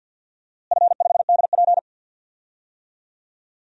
Morse code, heard as a keyed tone, about 52 wpm, 700 Hz. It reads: F5DP
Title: DXCC-100 Call Signs - 4 Characters - Repeated 3x - 50wpm